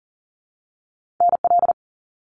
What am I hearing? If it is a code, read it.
Morse code, heard as a keyed tone, about 40 wpm, 700 Hz. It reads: NL
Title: Canadian Provinces and Territories - Repeated 3x - 40wpm